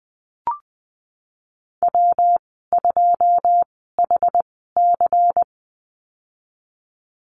Morse code, heard as a keyed tone, about 20 wpm, spaced as for 15 wpm, 700 Hz. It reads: W2HC
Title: 154 Lesson 25g - Letter p - Call Signs 20wpm